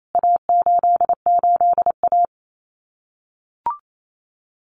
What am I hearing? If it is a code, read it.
Morse code, heard as a keyed tone, about 28 wpm, 700 Hz. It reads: A88A
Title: CWT Exchange 28wpm